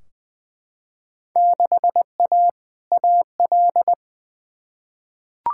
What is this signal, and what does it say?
Morse code, heard as a keyed tone, about 20 wpm, 700 Hz. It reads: 6A AL
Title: Field Day Exchanges Encore - 20wpm